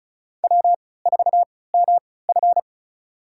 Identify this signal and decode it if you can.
Morse code, heard as a keyed tone, about 36 wpm, spaced as for 20 wpm, 700 Hz. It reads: W4MF
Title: USA Call Signs - 4 Characters - 3x Character Spacing - 35wpm